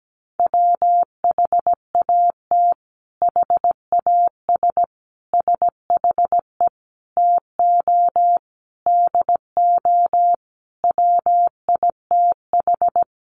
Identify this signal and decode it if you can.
Morse code, heard as a keyed tone, about 17 wpm, 700 Hz. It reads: WHAT HAS SHE TO DO WITH
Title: Sentences from Top 100 Words 17wpm